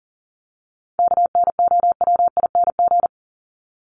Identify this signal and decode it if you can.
Morse code, heard as a keyed tone, about 40 wpm, 700 Hz. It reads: KNOWING
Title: ING Suffix - Top 100 Words with Base Words 40wpm